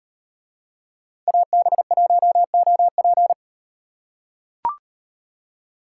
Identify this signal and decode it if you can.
Morse code, heard as a keyed tone, about 37 wpm, 700 Hz. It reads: AB1OP